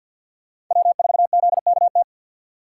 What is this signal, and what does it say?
Morse code, heard as a keyed tone, about 46 wpm, 700 Hz. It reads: W4ZKT
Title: USA Call Signs - 5 Characters - 4x Character Spacing - 50wpm